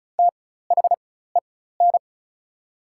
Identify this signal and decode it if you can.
Morse code, heard as a keyed tone, about 36 wpm, spaced as for 16 wpm, 700 Hz. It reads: THEN
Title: 4x Character Spacing - Top 300 Words - 35wpm